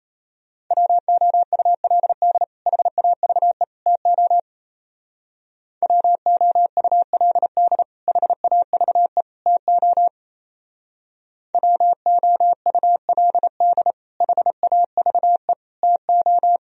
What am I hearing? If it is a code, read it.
Morse code, heard as a keyed tone, about 35 wpm, 700 Hz. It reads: WOULD HAVE TO WOULD HAVE TO WOULD HAVE TO
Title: Speed-Racing - Sets of 3 Words 28wpm